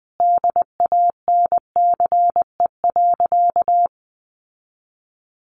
Morse code, heard as a keyed tone, about 20 wpm, 700 Hz. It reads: DANCE.
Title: Sentences from Top 900 Words 20wpm